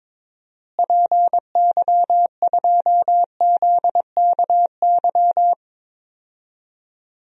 Morse code, heard as a keyed tone, about 21 wpm, 700 Hz. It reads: PY2ZKY
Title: DXCC-100 Call Signs - 6 Characters - Repeated 3x - 22wpm